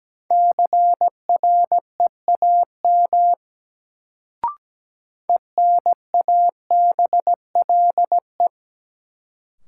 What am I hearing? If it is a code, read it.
Morse code, heard as a keyed tone, about 17 wpm, 700 Hz. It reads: CREAM ENABLE
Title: Sentences from Top 1500 Words - New Word Drill - 17wpm